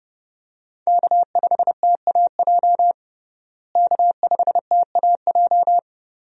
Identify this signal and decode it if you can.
Morse code, heard as a keyed tone, about 32 wpm, 700 Hz. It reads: K5TAJ K5TAJ
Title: USA Call Signs - 5 Characters - Repeated 3x - 30wpm